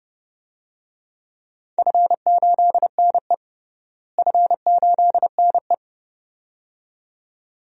Morse code, heard as a keyed tone, about 32 wpm, 700 Hz. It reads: F8NE F8NE
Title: DXCC-100 Call Signs - 4 Characters - Repeated 2x - 30wpm